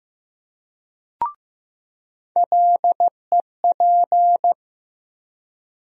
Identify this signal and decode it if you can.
Morse code, heard as a keyed tone, about 15 wpm, 700 Hz. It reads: LEP
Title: USA Call Sign Postfixes - 3 Letters 15wpm